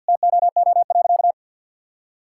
Morse code, heard as a keyed tone, about 50 wpm, 700 Hz. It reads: TOO.
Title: Sentences from Top 1000 Words - Review Part 5 - 50wpm